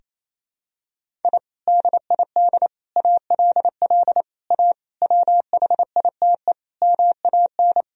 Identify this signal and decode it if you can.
Morse code, heard as a keyed tone, about 28 wpm, 700 Hz. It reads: I DID ALL A WHITE MAN
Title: Sentences from Top 1000 Words - Review Part 1 - 28wpm